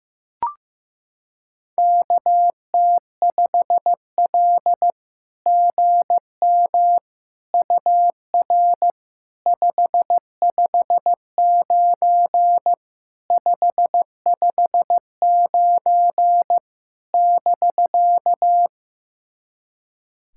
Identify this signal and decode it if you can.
Morse code, heard as a keyed tone, about 15 wpm, 700 Hz. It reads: KT5L GM UR 559 559 <BK>
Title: SOTA QSO Practice 15wpm